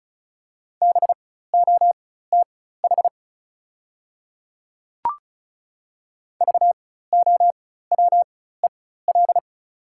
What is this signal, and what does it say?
Morse code, heard as a keyed tone, about 36 wpm, spaced as for 16 wpm, 690 Hz. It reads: BOTH VOWEL
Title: 4x Character Spacing - Top 500 Words - 35wpm